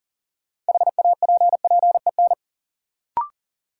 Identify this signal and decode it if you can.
Morse code, heard as a keyed tone, about 40 wpm, 700 Hz. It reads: HAPPEN
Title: Sets of 5 Words Encore - New Word Drill - 40wpm